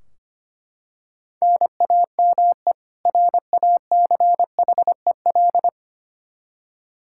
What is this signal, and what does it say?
Morse code, heard as a keyed tone, about 25 wpm, 700 Hz. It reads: NAME RACHEL
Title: Practice QSO - Exchange 1 - Beginning - 25wpm